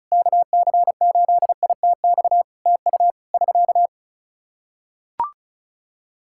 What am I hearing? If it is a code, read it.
Morse code, heard as a keyed tone, about 36 wpm, 700 Hz. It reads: KC8ITX TU <SK>
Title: K1USN SST QSO Practice 35wpm